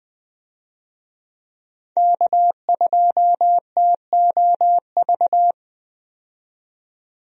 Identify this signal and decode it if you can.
Morse code, heard as a keyed tone, about 20 wpm, 700 Hz. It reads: K2TOV
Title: USA Call Signs - 5 Characters - 20wpm